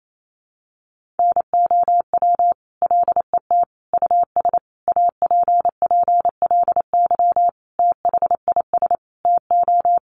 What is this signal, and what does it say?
Morse code, heard as a keyed tone, about 28 wpm, 700 Hz. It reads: NOW LET US APPLY THIS TO
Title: Sentences from Top 1200 Words 28wpm